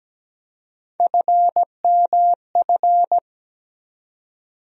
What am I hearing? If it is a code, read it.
Morse code, heard as a keyed tone, about 17 wpm, 700 Hz. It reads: FMF
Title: USA Call Sign Postfixes - 3 Letters 17wpm